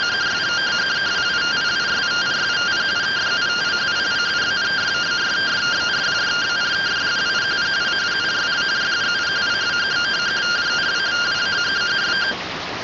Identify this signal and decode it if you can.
Two-tone teleprinter signal, about 170 Hz shift.
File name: findings / CT AXIAL -377.gen-17 rtty rx